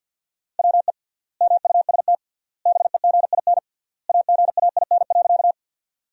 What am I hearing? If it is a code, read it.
Morse code, heard as a keyed tone, about 50 wpm, 700 Hz. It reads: WE MUST BEGIN AGAIN.
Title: 3x Word-Spacing - Sentences from Top 300 Words - 50wpm